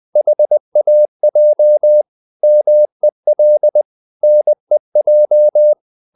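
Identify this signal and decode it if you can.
Morse code, heard as a keyed tone, about 20 wpm, 600 Hz. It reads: HAJ MEL NEJ
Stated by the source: Morse code practice of short Danish words at 20 wpm, from CW med Gnister.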